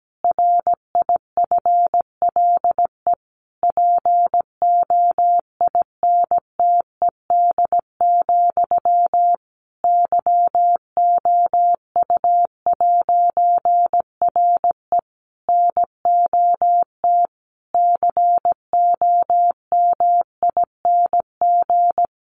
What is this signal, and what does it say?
Morse code, heard as a keyed tone, about 17 wpm, 700 Hz. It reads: RIFLE POINTED, YOU'RE NOT COMING
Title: A Sound Of Thunder 17wpm